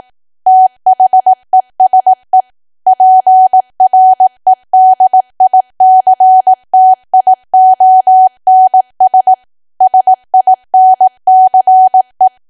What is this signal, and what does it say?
Morse code, heard as a keyed tone, about 18 wpm, 750 Hz. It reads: THESE PREDICTIONS SINCE
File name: cwa / W1AW-Level2 / 18WPM